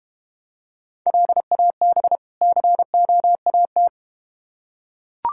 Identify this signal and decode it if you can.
Morse code, heard as a keyed tone, about 32 wpm, 700 Hz. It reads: LAB COAT